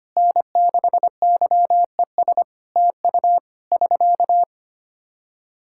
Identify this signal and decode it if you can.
Morse code, heard as a keyed tone, about 25 wpm, 700 Hz. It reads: N6YES TU <SK>
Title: Practice QSO - Exchange 4 - Ending - 25wpm